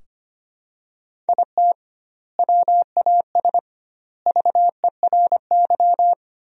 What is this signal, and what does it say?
Morse code, heard as a keyed tone, about 25 wpm, 700 Hz. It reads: IT WAS VERY
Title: 2x Word-Spacing - Sentences from Top 500 Words - 25wpm